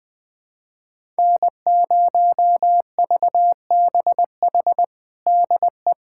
Morse code, heard as a keyed tone, about 21 wpm, 700 Hz. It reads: N0VBH DE
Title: USA Callsign DE Callsign 20wpm